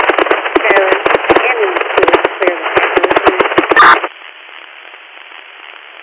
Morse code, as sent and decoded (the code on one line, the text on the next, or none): none